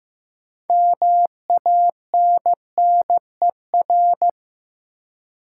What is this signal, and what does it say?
Morse code, heard as a keyed tone, about 15 wpm, 700 Hz. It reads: MANNER
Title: Six Letter Words - 15wpm